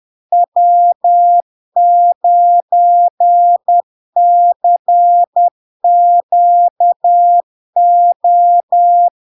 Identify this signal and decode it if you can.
Morse code, heard as a keyed tone, about 10 wpm, 700 Hz. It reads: W9CQO
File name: LARK-FD-2022-Log-in-CW-10wpm-700-Hz